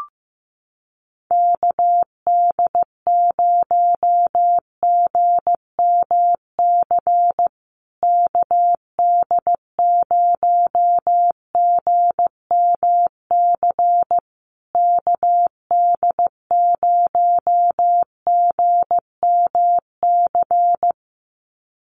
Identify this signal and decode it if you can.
Morse code, heard as a keyed tone, about 15 wpm, 700 Hz. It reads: KD0GMC KD0GMC KD0GMC
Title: USA Call Signs Repeated 3x 15wpm